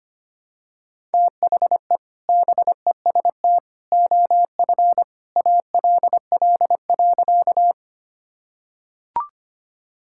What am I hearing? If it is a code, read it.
Morse code, heard as a keyed tone, about 25 wpm, 700 Hz. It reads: THE BEST OF ALL.
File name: practice / 25wpm / Sentences with 2-Word Phrases - 25wpm